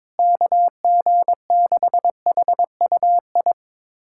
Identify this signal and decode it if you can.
Morse code, heard as a keyed tone, about 22 wpm, 700 Hz. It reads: KG6HUI
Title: USA Callsign DE Callsign 22wpm